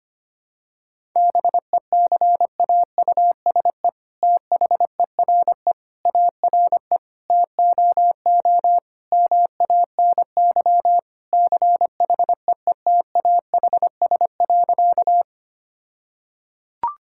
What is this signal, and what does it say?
Morse code, heard as a keyed tone, about 25 wpm, 700 Hz. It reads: BECAUSE THERE ARE TOO MANY CHEETAHS.
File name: practice / 25wpm / All-Age Appropriate Jokes - 25wpm